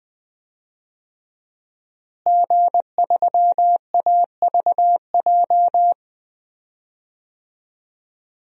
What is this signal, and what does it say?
Morse code, heard as a keyed tone, about 20 wpm, 700 Hz. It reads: G3AVJ
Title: DXCC-100 Call Signs - 5 Characters - Repeated 2x - 20wpm